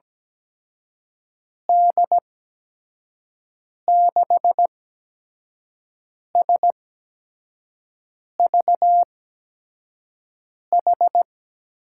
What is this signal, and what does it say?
Morse code, heard as a keyed tone, about 17 wpm, spaced as for 5 wpm, 700 Hz. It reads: D6SVH
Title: USA Call Signs - 6 Characters - 8x Character Spacing - 17wpm